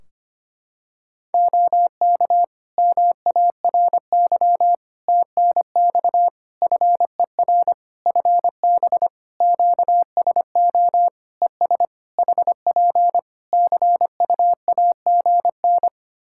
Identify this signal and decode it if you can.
Morse code, heard as a keyed tone, about 25 wpm, 700 Hz. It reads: OK MARY TNX FER FB QSO ES HP CUAGN
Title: Practice QSO - Exchange 4 - Ending - 25wpm